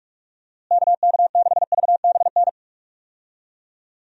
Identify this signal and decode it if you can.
Morse code, heard as a keyed tone, about 46 wpm, 700 Hz. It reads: KK6VBN
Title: USA Call Signs - 6 Characters - Repeated 3x - 45wpm